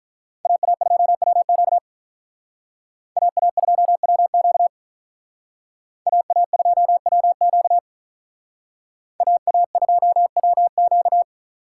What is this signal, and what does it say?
Morse code, heard as a keyed tone, about 45 wpm, 700 Hz. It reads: AA2WQ AA2WQ AA2WQ AA2WQ